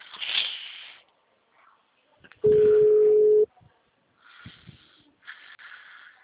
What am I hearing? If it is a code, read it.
Ringback tone (Europe, 425 Hz).